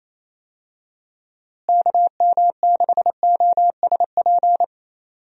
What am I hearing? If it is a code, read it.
Morse code, heard as a keyed tone, about 27 wpm, 700 Hz. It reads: KM6OSP
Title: USA Call Signs - 6 Characters - Repeated 2x - 28wpm